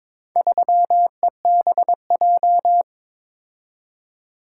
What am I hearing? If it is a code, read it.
Morse code, heard as a keyed tone, about 22 wpm, 700 Hz. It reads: 3EBJ
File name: practice / 22wpm / Four Letter-Number Combinations - 22wpm